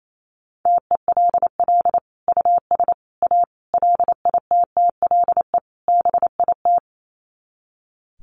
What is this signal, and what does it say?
Morse code, heard as a keyed tone, about 28 wpm, 700 Hz. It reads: TELL US A LITTLE BIT
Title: Sets of 5 Words Encore2 28wpm